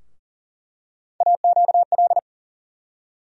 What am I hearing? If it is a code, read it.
Morse code, heard as a keyed tone, about 40 wpm, 700 Hz. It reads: AQL
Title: USA Call Sign Postfixes - 3 Letters 40wpm